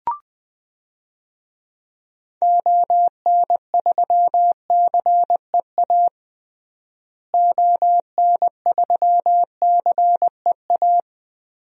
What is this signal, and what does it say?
Morse code, heard as a keyed tone, about 20 wpm, 700 Hz. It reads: ON3CEA ON3CEA
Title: DXCC-100 Call Signs - 6 Characters - Repeated 3x - 20wpm